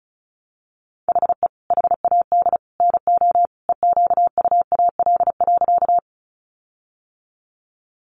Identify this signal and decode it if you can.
Morse code, heard as a keyed tone, about 34 wpm, 700 Hz. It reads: HE HAD NO EQUAL.
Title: Sentences from Top 1400 Words 35wpm